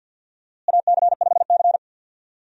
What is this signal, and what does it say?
Morse code, heard as a keyed tone, about 51 wpm, 700 Hz. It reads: AC5C
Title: USA Call Signs - 4 Characters - Repeated 2x - 50wpm